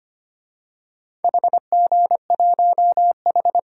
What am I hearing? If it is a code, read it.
Morse code, heard as a keyed tone, about 25 wpm, 700 Hz. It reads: HG1H
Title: DXCC-100 Call Signs - 4 Characters - 25wpm